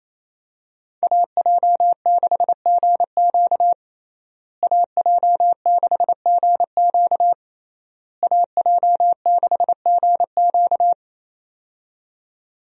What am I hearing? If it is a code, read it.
Morse code, heard as a keyed tone, about 28 wpm, 700 Hz. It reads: AJ6GQ AJ6GQ AJ6GQ
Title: USA Call Signs - 5 Characters - Repeated 3x - 28wpm